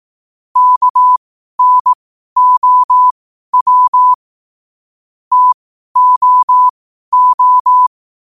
Morse code, heard as a keyed tone, about 18 wpm, spaced as for 12 wpm, 1000 Hz. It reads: KNOW TOO
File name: qotdfiles / 12wpm